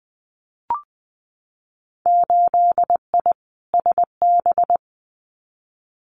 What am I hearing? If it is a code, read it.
Morse code, heard as a keyed tone, about 19 wpm, 700 Hz. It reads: 8I SB
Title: Winter Field Day Exchanges Encore 20wpm